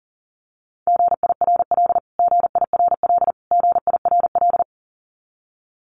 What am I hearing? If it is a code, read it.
Morse code, heard as a keyed tone, about 40 wpm, 700 Hz. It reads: GIRL GIRL GIRL
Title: Top 500 Words Repeated 40wpm